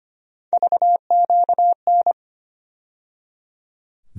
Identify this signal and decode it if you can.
Morse code, heard as a keyed tone, about 25 wpm, 700 Hz. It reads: VQN
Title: USA Call Sign Postfixes - 3 Letters 25wpm